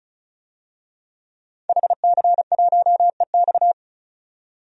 Morse code, heard as a keyed tone, about 35 wpm, 700 Hz. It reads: HC1EX
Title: DXCC-100 Call Signs - 5 Characters - 35wpm